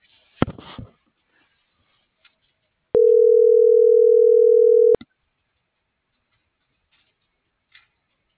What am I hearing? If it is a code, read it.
Ringback tone (North America), 440 + 480 Hz.